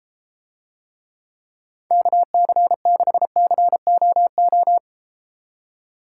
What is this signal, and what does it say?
Morse code, heard as a keyed tone, about 33 wpm, 700 Hz. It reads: KC6COO